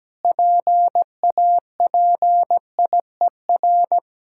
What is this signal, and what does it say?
Morse code, heard as a keyed tone, about 17 wpm, 700 Hz. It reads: PAPIER
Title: Pangram Challenge v2 - 17wpm